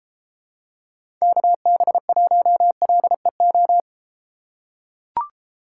Morse code, heard as a keyed tone, about 31 wpm, 700 Hz. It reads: KB1LEO